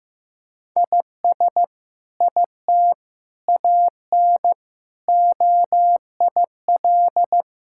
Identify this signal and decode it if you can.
Morse code, heard as a keyed tone, about 15 wpm, 700 Hz. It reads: IS IT AN OIL
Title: Sentences from Top 1000 Words - Review Part 6 - 15wpm